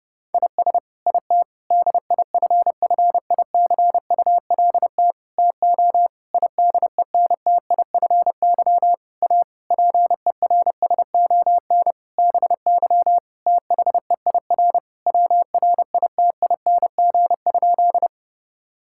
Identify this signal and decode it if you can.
Morse code, heard as a keyed tone, about 30 wpm, 700 Hz. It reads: IS IT DIFFICULT TO IDENTIFY A PERSON BY THEIR WRITING?
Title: Sentences from Top 1000 Words - Review Part 6 - 30wpm